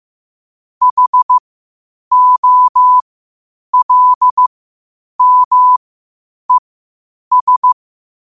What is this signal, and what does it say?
Morse code, heard as a keyed tone, about 15 wpm, spaced as for 8 wpm, 1000 Hz. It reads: HOLMES
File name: qotdfiles / 8wpm